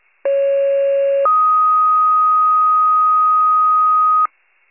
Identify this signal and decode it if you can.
Two-tone page, heard as a one-second tone, then a three-second tone, about 550 then 1250 Hz. It is heard from 0.3 to 4.3 s.
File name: wcalert